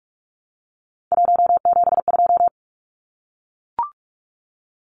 Morse code, heard as a keyed tone, about 44 wpm, 700 Hz. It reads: 172
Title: Numbers - 3-Digit 45wpm